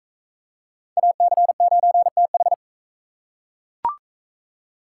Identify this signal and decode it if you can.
Morse code, heard as a keyed tone, about 40 wpm, 700 Hz. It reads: AC9TH